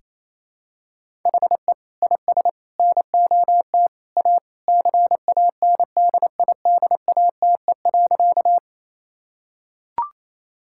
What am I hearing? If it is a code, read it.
Morse code, heard as a keyed tone, about 28 wpm, 700 Hz. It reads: HE IS NOT A CANDIDATE.
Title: Sentences from Top 1600 Words 28wpm